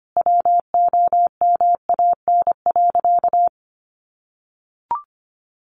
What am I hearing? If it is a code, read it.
Morse code, heard as a keyed tone, about 25 wpm, 700 Hz. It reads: WOMAN.
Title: Sentences from Top 1400 Words 25wpm